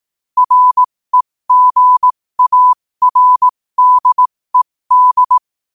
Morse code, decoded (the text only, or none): REGARDED